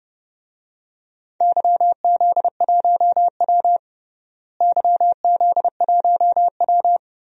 Morse code, heard as a keyed tone, about 30 wpm, 700 Hz. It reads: YZ1W YZ1W
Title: DXCC-100 Call Signs - 4 Characters - Repeated 3x - 30wpm